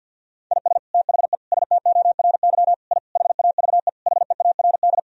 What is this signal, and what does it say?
Morse code, heard as a keyed tone, about 52 wpm, 700 Hz. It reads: IS THE STORY I HAVE HEARD
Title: Sentences from Top 1000 Words - Review Part 2 - 50wpm